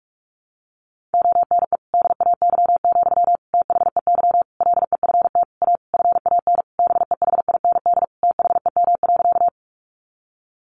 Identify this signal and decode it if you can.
Morse code, heard as a keyed tone, about 46 wpm, 700 Hz. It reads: ONE DAY, THEY LEFT A FAN BEHIND THEM.
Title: Sentences from Top 1300 Words 45wpm